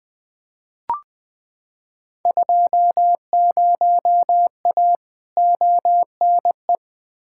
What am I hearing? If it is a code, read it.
Morse code, heard as a keyed tone, about 21 wpm, 700 Hz. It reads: 20A ONE
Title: Field Day Exchanges Encore - 20wpm